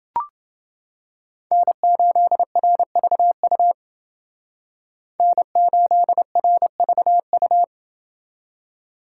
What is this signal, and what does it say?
Morse code, heard as a keyed tone, about 28 wpm, 700 Hz. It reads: N8RVU N8RVU